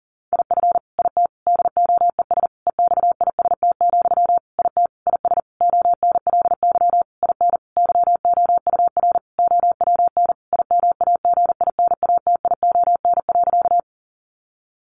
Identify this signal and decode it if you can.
Morse code, heard as a keyed tone, about 40 wpm, 700 Hz. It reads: IF IT DOES EXIST, IT IS ONLY IN YOUR OWN IMAGINATION.
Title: Sentences from Top 1300 Words 40wpm